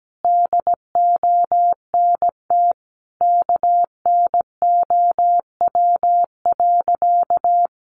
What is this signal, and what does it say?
Morse code, heard as a keyed tone, about 17 wpm, 700 Hz. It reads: DONT KNOW.
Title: Sentences from Top 600 Words 17wpm